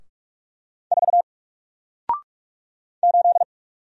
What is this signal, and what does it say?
Morse code, heard as a keyed tone, about 46 wpm, 700 Hz. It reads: 4 8
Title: Single Character (Letter:Number:Punctuation:Prosign) 45wpm